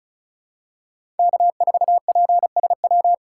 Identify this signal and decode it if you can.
Morse code, heard as a keyed tone, about 36 wpm, 700 Hz. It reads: K4PSW
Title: USA Call Signs - 5 Characters - 3x Character Spacing - 35wpm